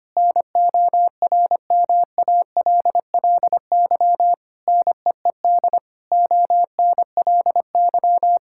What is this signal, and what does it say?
Morse code, heard as a keyed tone, about 25 wpm, 700 Hz. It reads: NORMALLY NEED ONLY
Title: Sentences from Top 1000 Words - Review Part 6 - 25wpm